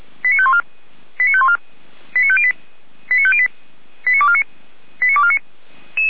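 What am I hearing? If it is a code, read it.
Five-tone selective call (ZVEI), heard as quick runs of five tones, tones of about 70 ms. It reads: 89514 89514 89508 89508 89249 89249
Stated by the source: ZVEI five-tone selective call 89250, received by SDR.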